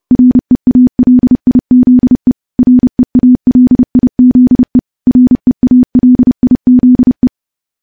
Morse code, heard as a keyed tone, about 30 wpm, 250 Hz. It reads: REALIZE REALIZE REALIZE
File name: morse-training-30WPM